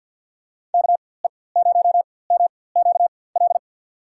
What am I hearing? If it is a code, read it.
Morse code, heard as a keyed tone, about 53 wpm, spaced as for 24 wpm, 700 Hz. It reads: KE0MQL
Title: USA Call Signs - 6 Characters - 4x Character Spacing - 50wpm